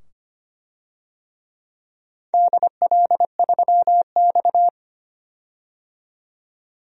Morse code, heard as a keyed tone, about 25 wpm, 700 Hz. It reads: DL3X
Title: DXCC-100 Call Signs - 4 Characters - 25wpm